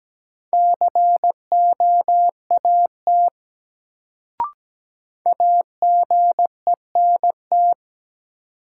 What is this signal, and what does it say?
Morse code, heard as a keyed tone, about 17 wpm, 700 Hz. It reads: COAT AGENT
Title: Sentences from Top 1600 Words - New Word Drill - 17wpm